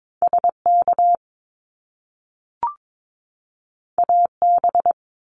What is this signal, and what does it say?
Morse code, heard as a keyed tone, about 22 wpm, 700 Hz. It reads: SK AB
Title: Canadian Provinces and Territories 22wpm